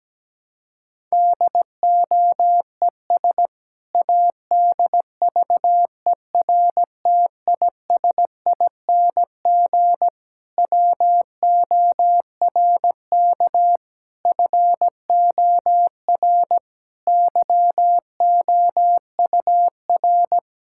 Morse code, heard as a keyed tone, about 17 wpm, 700 Hz. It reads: DOES ADVERTISING WORK FOR YOUR
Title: Sentences from Top 1000 Words - Review Part 6 - 17wpm